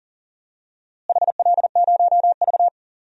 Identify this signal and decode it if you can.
Morse code, heard as a keyed tone, about 37 wpm, 700 Hz. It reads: HL0V
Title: DXCC-100 Call Signs - 4 Characters - Repeated 3x - 40wpm